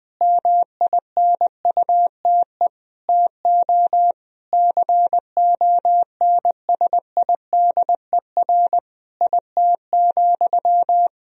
Morse code, heard as a keyed tone, about 20 wpm, 700 Hz. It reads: MINUTE TO CONSIDER IT,
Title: Sentences from Top 1000 Words - Review Part 4 - 20wpm